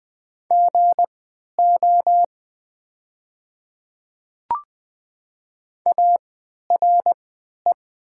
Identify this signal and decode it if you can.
Morse code, heard as a keyed tone, about 20 wpm, spaced as for 11 wpm, 700 Hz. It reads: GO ARE
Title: 3x Character Spacing - Top 300 Words - 20wpm